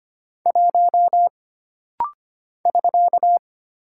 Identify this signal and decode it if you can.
Morse code, heard as a keyed tone, about 25 wpm, 700 Hz. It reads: 1 <SK>
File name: Single Character (Letter:Number:Punctuation:Prosign) - Rapid-Fire - 25wpm